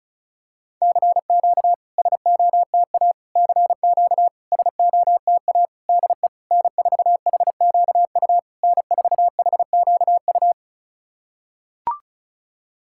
Morse code, heard as a keyed tone, about 36 wpm, 690 Hz. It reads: CQ SOTA CQ SOTA DE N4HQU N4HQU
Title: SOTA QSO Practice 35wpm